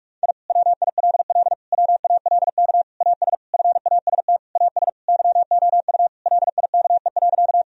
Morse code, heard as a keyed tone, about 46 wpm, 700 Hz. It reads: I WILL WALK AS FAST AS YOU LIKE.
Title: Sentences from Top 1000 Words - Review Part 2 - 45wpm